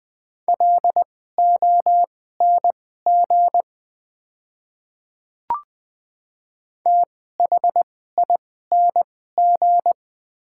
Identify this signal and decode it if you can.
Morse code, heard as a keyed tone, about 20 wpm, spaced as for 14 wpm, 700 Hz. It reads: LONG THING
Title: 2x Character Spacing - Top 200 Words - 20wpm